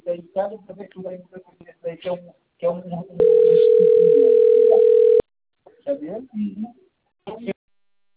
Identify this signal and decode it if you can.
Ringback tone (North America), 440 + 480 Hz, with speech.
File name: rg-1600-unknown- - - .88544